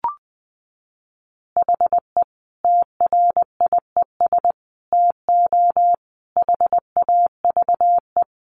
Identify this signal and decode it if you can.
Morse code, heard as a keyed tone, about 20 wpm, 700 Hz. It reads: HE TRIES TO HAVE
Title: Easy Phrases 20wpm